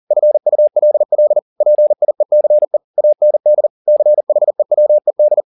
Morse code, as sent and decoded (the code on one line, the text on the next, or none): ..-. ..- .-.. .-.. / .--. .. -.-. / .- -. -.. / -.-. .... .-- . -..
FULL PIECE AND CHEWED